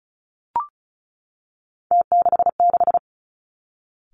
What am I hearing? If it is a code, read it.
Morse code, heard as a keyed tone, about 36 wpm, 700 Hz. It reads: T66
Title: Three Challenging Letter-Number Combinations 35wpm